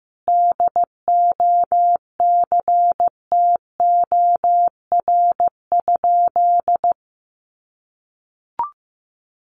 Morse code, heard as a keyed tone, about 15 wpm, 700 Hz. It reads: DOCTOR?
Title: Sentences from Top 1500 Words 15wpm